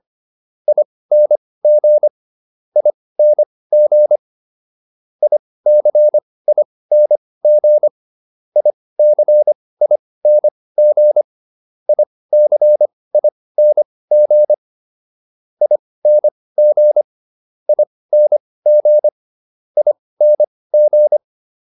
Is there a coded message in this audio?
Yes: Morse code from 0.7 to 21.2 s, heard as a keyed tone, about 25 wpm, spaced as for 18 wpm, 600 Hz. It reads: ING ING ICING ICING ICING ING ING ING